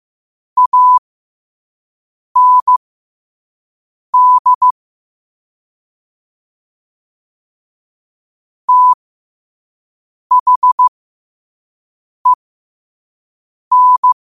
Morse code, heard as a keyed tone, about 15 wpm, spaced as for 5 wpm, 1000 Hz. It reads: AND THEN
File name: qotdfiles / 5wpm